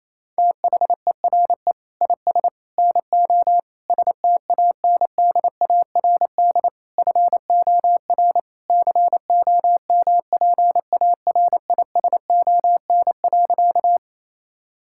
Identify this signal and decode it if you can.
Morse code, heard as a keyed tone, about 27 wpm, 700 Hz. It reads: THERE IS NO STANDARD FOR COMPARISON.
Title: Sentences from Top 1600 Words 28wpm